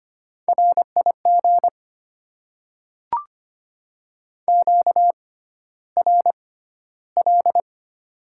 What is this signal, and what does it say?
Morse code, heard as a keyed tone, about 25 wpm, spaced as for 13 wpm, 690 Hz. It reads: RIG QRL
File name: Top 20 QSO Elements - 6x Character Spacing - 25wpm